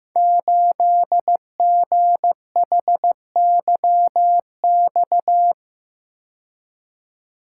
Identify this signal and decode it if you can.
Morse code, heard as a keyed tone, about 15 wpm, 700 Hz. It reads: 8GHYX
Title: Code Groups - 5 Letter:Numbers - 15wpm